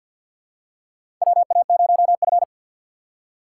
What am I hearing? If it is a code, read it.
Morse code, heard as a keyed tone, about 47 wpm, 700 Hz. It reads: WA0F
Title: USA Call Signs - 4 Characters - 6x Character Spacing - 50wpm